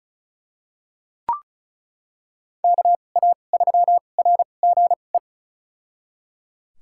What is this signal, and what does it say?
Morse code, heard as a keyed tone, about 34 wpm, spaced as for 25 wpm, 700 Hz. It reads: KA3RGE
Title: USA Call Signs - 6 Characters - 2x Character Spacing - 35wpm